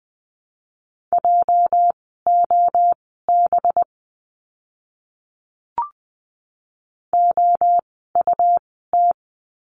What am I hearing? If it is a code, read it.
Morse code, heard as a keyed tone, about 20 wpm, spaced as for 14 wpm, 700 Hz. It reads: JOB OUT
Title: 206 Lesson 33f - Letter j - Words 20wpm